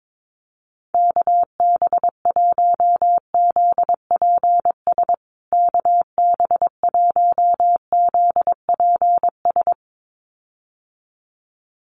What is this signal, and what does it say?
Morse code, heard as a keyed tone, about 22 wpm, 700 Hz. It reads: KB1ZPS KB1ZPS
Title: USA Call Signs Repeated 2x 22wpm